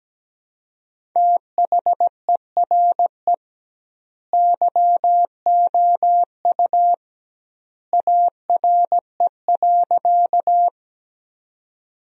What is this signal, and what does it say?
Morse code, heard as a keyed tone, about 17 wpm, 700 Hz. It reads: THERE YOU ARE.
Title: 2x Word-Spacing - Sentences from Top 100 Words - 17wpm